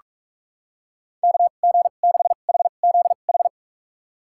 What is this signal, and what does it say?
Morse code, heard as a keyed tone, about 46 wpm, spaced as for 33 wpm, 700 Hz. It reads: KG6HZH
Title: USA Call Signs - 6 Characters - 2x Character Spacing - 45wpm